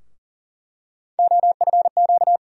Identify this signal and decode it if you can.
Morse code, heard as a keyed tone, about 40 wpm, 700 Hz. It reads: OFQ
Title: USA Call Sign Postfixes - 3 Letters 40wpm